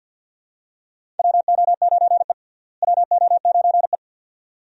Morse code, heard as a keyed tone, about 52 wpm, 700 Hz. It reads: WO9E WO9E
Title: USA Call Signs - 4 Characters - Repeated 2x - 50wpm